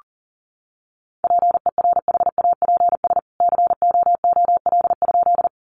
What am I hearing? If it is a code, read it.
Morse code, heard as a keyed tone, about 40 wpm, 700 Hz. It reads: PERHAPS COOL?
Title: Sentences from Top 1200 Words 40wpm